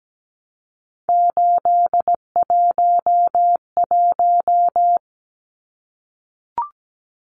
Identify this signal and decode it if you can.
Morse code, heard as a keyed tone, about 17 wpm, 700 Hz. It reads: 811
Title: Numbers - 3-Digit 17wpm